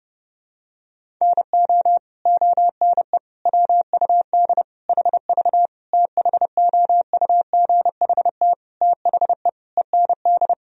Morse code, heard as a keyed tone, about 30 wpm, 700 Hz. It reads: NO ONE WUD HV THOUGHT THE END
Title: Sentences from Top 300 Words and Abbreviations 30wpm